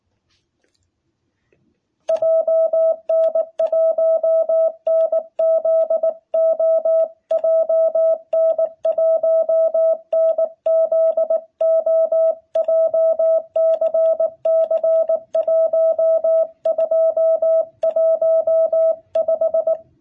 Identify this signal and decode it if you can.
Morse code, heard as a keyed tone, about 19 wpm, 660 Hz. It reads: JN1NZO JN1NZO JCC 1215
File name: jh8cbh / 2021cw